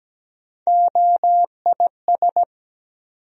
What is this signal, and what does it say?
Morse code, heard as a keyed tone, about 17 wpm, 700 Hz. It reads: OIS
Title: USA Call Sign Postfixes - 3 Letters 17wpm